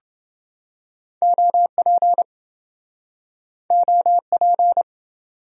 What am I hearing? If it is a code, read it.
Morse code, heard as a keyed tone, about 26 wpm, 700 Hz. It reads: OP OP